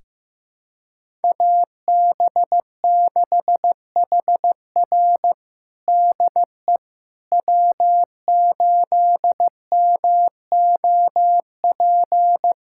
Morse code, heard as a keyed tone, about 15 wpm, 700 Hz. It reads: AB6HR DE W8MOP
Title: USA Callsign DE Callsign - 5 Characters - 15wpm